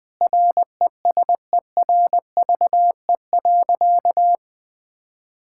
Morse code, heard as a keyed tone, about 20 wpm, 700 Hz. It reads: RESERVE.
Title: Sentences from Top 1000 Words - Review Part 6 - 20wpm